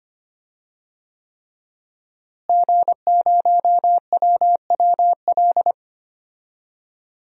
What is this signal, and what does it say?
Morse code, heard as a keyed tone, about 25 wpm, 700 Hz. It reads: G0WWL
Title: DXCC-100 Call Signs - 5 Characters - Repeated 3x - 25wpm